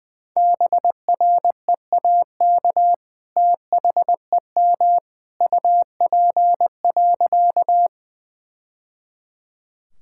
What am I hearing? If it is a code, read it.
Morse code, heard as a keyed tone, about 20 wpm, 700 Hz. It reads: BREAK THEM UP.
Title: Sentences from Top 1000 Words - Review Part 6 - 20wpm